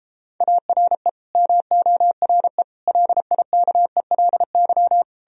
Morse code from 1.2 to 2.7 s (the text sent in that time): MORE